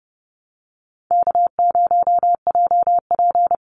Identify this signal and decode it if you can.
Morse code, heard as a keyed tone, about 30 wpm, 700 Hz. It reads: K0JP
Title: Easy Call Signs - 30wpm